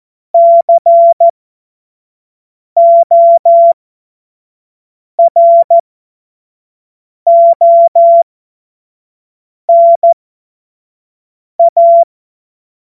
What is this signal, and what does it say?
Morse code, heard as a keyed tone, about 14 wpm, spaced as for 5 wpm, 680 Hz. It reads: CORONA
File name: Morsekurs / Witze / Witz 20. Juli 2021, 25 F70